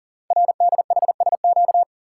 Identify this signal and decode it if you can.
Morse code, heard as a keyed tone, about 40 wpm, 690 Hz. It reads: RDHSQ
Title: Code Groups - 5 Letters - 40wpm